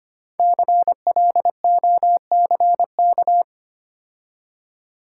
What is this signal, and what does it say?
Morse code, heard as a keyed tone, about 28 wpm, 690 Hz. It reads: CLOCK
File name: Easy Words - Speed-Racing with Hints - 25wpm